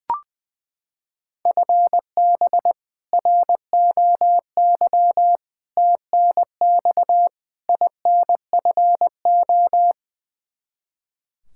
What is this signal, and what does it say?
Morse code, heard as a keyed tone, about 21 wpm, 700 Hz. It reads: FB ROY TNX INFO
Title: K1USN SST QSO Practice 20wpm